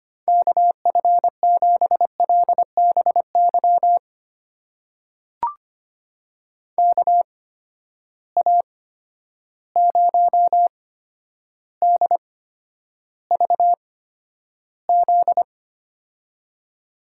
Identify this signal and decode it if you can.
Morse code, heard as a keyed tone, about 25 wpm, spaced as for 11 wpm, 700 Hz. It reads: KF7LBY KA0DVZ
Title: USA Call Signs - 6 Characters - 8x Character Spacing - 25wpm